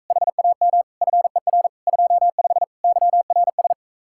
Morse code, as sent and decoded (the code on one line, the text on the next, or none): .... .- -- / ..-. .-. / ..--- ..... / -.-- .-. ...
HAM FER 25 YRS